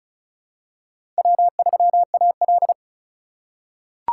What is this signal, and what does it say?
Morse code, heard as a keyed tone, about 36 wpm, 700 Hz. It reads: W3AL
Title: USA Call Signs - 4 Characters - 4x Character Spacing - 35wpm